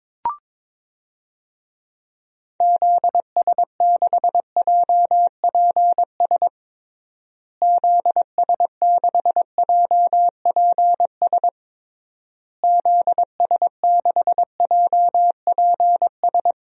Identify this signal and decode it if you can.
Morse code, heard as a keyed tone, about 22 wpm, 700 Hz. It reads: ZS6JPS ZS6JPS ZS6JPS
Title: DXCC-100 Call Signs - 6 Characters - Repeated 3x - 22wpm